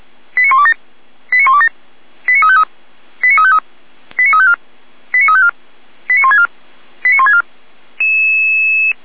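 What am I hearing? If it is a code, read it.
Five-tone selective call (ZVEI), heard as quick runs of five tones, tones of about 70 ms. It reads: 89127 89127 89353 89353 89354 89354 89164 89164
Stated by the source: ZVEI five-tone selective call received with an SDR.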